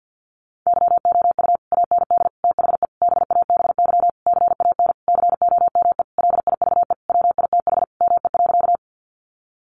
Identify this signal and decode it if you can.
Morse code, heard as a keyed tone, about 48 wpm, 700 Hz. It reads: YOU AND THE BABY CAN COME LIVE WITH ME.
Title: Sentences from Top 1300 Words 50wpm